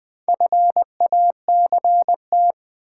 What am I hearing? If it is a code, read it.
Morse code, heard as a keyed tone, about 20 wpm, 700 Hz. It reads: FACT
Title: Sets of 2 Words - New Word Drill - 20wpm